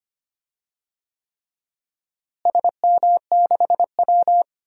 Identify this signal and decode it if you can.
Morse code, heard as a keyed tone, about 25 wpm, 700 Hz. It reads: SM6W
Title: DXCC-100 Call Signs - 4 Characters - Repeated 3x - 25wpm